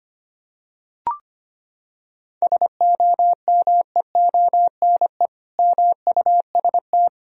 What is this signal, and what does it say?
Morse code, heard as a keyed tone, about 25 wpm, 700 Hz. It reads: SOMEONE MUST
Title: Sentences from Top 1000 Words - Review Part 4 - 25wpm